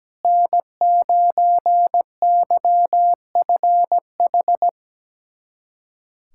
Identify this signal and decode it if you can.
Morse code, heard as a keyed tone, about 17 wpm, 700 Hz. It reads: N9YFH
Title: USA Callsign DE Callsign - 5 Characters - 17wpm